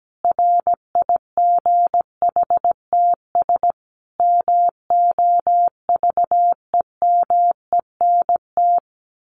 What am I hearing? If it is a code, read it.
Morse code, heard as a keyed tone, about 17 wpm, 700 Hz. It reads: RIGHTS MOVEMENT
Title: Sets of 5 Words Encore2 17wpm